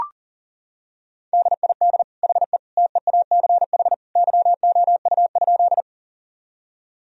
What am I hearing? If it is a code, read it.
Morse code, heard as a keyed tone, about 37 wpm, 700 Hz. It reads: DID HE TEACH YOU?
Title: Sentences from Top 1000 Words - Review Part 2 - 40wpm